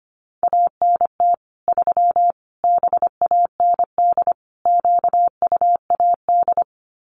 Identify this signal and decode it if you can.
Morse code, heard as a keyed tone, about 25 wpm, 700 Hz. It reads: ANT 3 BAND QUAD
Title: Common QSO Phrases 25wpm